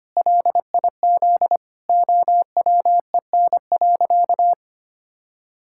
Morse code, heard as a keyed tone, about 25 wpm, 700 Hz. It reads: LIZ OWEN.
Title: Pangram Challenge v2 - 25wpm